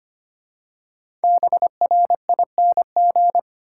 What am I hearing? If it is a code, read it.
Morse code, heard as a keyed tone, about 25 wpm, 700 Hz. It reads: BRING
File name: 5 Letter Words - First 3 Letters Given - 25wpm